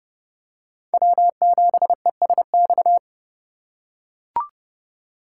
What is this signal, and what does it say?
Morse code, heard as a keyed tone, about 30 wpm, 700 Hz. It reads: W7ESX
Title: USA Call Signs - 5 Characters - 2x Character Spacing - 30wpm